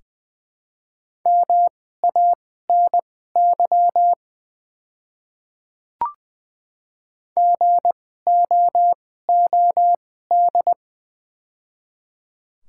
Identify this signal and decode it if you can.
Morse code, heard as a keyed tone, about 20 wpm, spaced as for 14 wpm, 700 Hz. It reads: MANY GOOD
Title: 2x Character Spacing - Top 200 Words - 20wpm